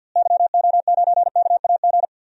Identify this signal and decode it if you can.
Morse code, heard as a keyed tone, about 51 wpm, 700 Hz. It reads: YO9KAG
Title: DXCC-100 Call Signs - 6 Characters - Repeated 3x - 50wpm